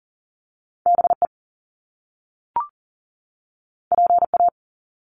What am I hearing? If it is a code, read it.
Morse code, heard as a keyed tone, about 38 wpm, 700 Hz. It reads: BE PA
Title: Top Two Letter Combinations in Words 40wpm